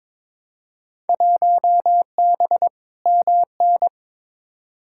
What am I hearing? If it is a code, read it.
Morse code, heard as a keyed tone, about 22 wpm, 700 Hz. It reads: 1B MN
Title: Field Day Exchanges Encore - 22wpm